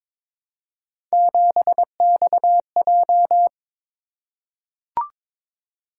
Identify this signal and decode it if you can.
Morse code, heard as a keyed tone, about 23 wpm, 700 Hz. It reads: 7XJ
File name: practice / 22wpm / Three Letter-Number Combinations - 22wpm